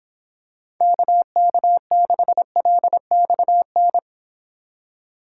Morse code, heard as a keyed tone, about 26 wpm, 700 Hz. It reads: KK6LXN